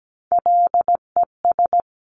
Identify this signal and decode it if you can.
Morse code, heard as a keyed tone, about 17 wpm, 700 Hz. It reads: LES
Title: CWT Exchange 17wpm